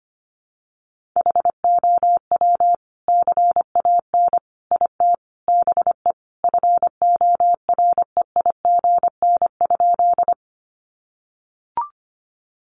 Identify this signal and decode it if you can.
Morse code, heard as a keyed tone, about 25 wpm, 700 Hz. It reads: HOW CAN IT BE FOREIGN?
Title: Sentences from Top 1400 Words 25wpm